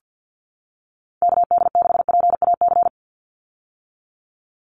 Morse code, heard as a keyed tone, about 48 wpm, 700 Hz. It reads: KD6PAC
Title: USA Call Signs 50wpm